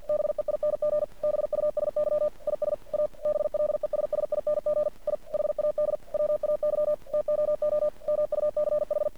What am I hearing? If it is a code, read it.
Morse code, heard as a keyed tone, about 48 wpm, 600 Hz. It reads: BEING BUSY IS A BLESSING I HAD WAY TOO MUCH
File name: archived / CRSnet